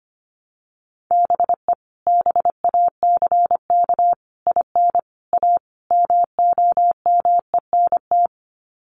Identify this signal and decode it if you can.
Morse code, heard as a keyed tone, about 25 wpm, 700 Hz. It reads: BE BACK IN A MOMENT
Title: Sets of 5 Words 25wpm